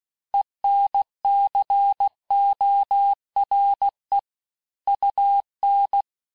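Morse code, heard as a keyed tone, about 16 wpm, 800 Hz. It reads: ENCORE UN
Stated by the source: Morse course text, end of QSO, sent at 15 words per minute.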